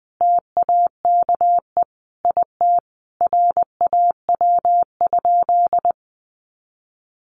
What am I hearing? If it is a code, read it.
Morse code, heard as a keyed tone, about 19 wpm, 700 Hz. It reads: TAKE IT RAW?
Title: Sentences from Top 1500 Words 20wpm